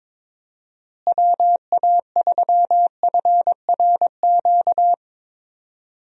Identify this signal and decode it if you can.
Morse code, heard as a keyed tone, about 22 wpm, 700 Hz. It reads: WA3FRQ
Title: USA Call Signs - 6 Characters - Repeated 2x - 22wpm